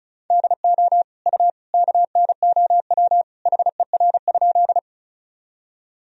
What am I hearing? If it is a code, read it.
Morse code, heard as a keyed tone, about 34 wpm, 700 Hz. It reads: DO U KNOW HER?
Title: Sentences from Top 100 Words and Abbreviations 35wpm